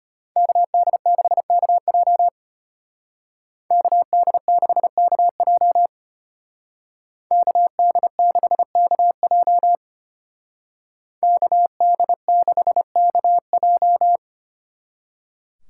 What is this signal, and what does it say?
Morse code, heard as a keyed tone, about 33 wpm, 700 Hz. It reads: KD6KJ KD6KJ KD6KJ KD6KJ